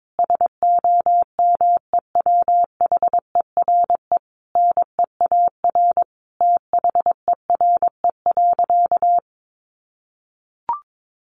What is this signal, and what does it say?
Morse code, heard as a keyed tone, about 22 wpm, 700 Hz. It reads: SOMEWHERE NEAR THERE.
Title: Sentences from Top 1400 Words 22wpm